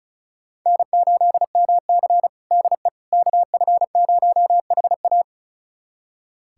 Morse code, heard as a keyed tone, about 36 wpm, 700 Hz. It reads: N8MC DE KF0HA
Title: USA Callsign DE Callsign 35wpm